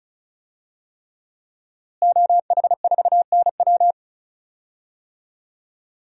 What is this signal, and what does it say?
Morse code, heard as a keyed tone, about 36 wpm, 700 Hz. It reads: OH4NW
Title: DXCC-100 Call Signs - 5 Characters - Repeated 3x - 35wpm